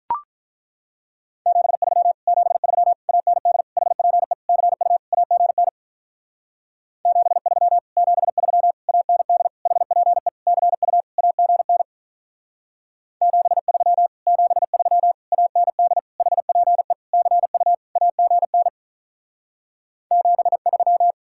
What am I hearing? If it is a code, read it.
Morse code, heard as a keyed tone, about 43 wpm, 700 Hz. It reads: 73 73 AND HPE CU AGN 73 73 AND HPE CU AGN 73 73 AND HPE CU AGN 73
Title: Speed-Racing Common QSO Phrases Encore - 35wpm